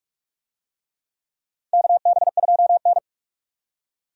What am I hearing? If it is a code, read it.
Morse code, heard as a keyed tone, about 46 wpm, 700 Hz. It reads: KB2N
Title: USA Call Signs - 4 Characters - 45wpm